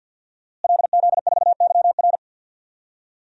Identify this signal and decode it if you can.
Morse code, heard as a keyed tone, about 50 wpm, 700 Hz. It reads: LZ4YR
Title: DXCC-100 Call Signs - 5 Characters - Repeated 3x - 50wpm